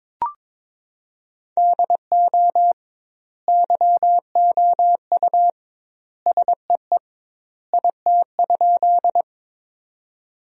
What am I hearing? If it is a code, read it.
Morse code, heard as a keyed tone, about 22 wpm, 700 Hz. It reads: DO YOU SEE IT?
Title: 2x Word-Spacing - Sentences from Top 100 Words - 22wpm